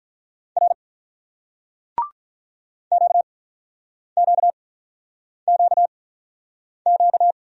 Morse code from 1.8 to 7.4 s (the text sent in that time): QQQQ